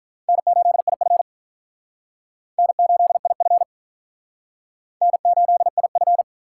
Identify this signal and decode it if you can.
Morse code, heard as a keyed tone, about 45 wpm, 700 Hz. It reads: N8IF N8IF N8IF